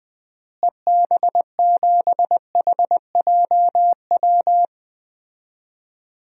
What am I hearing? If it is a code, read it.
Morse code, heard as a keyed tone, about 20 wpm, 700 Hz. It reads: EB7HJW
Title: DXCC-100 Call Signs - 6 Characters - Repeated 2x - 20wpm